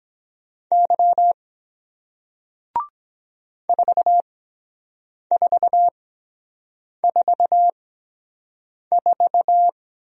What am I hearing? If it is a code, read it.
Morse code, heard as a keyed tone, about 23 wpm, spaced as for 7 wpm, 700 Hz. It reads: Y 4444